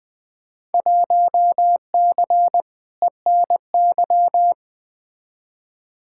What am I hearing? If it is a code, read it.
Morse code, heard as a keyed tone, about 20 wpm, 700 Hz. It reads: 1C ENY
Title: Field Day Exchanges Encore - 20wpm